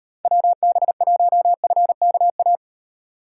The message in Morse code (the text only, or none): WB1FKA